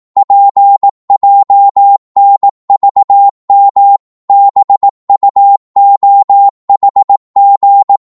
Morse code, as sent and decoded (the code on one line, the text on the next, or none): .--. .--- -. ...- -- / -... ..- --- .... --.
PJNVM BUOHG